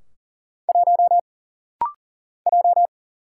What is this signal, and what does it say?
Morse code, heard as a keyed tone, about 39 wpm, 700 Hz. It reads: 1 J
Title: Single Letter-Number - Rapid-Fire 40wpm